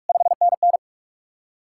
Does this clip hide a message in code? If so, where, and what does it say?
Yes: Morse code from 0.1 to 0.8 s, heard as a keyed tone, about 58 wpm, 700 Hz. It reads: 5NN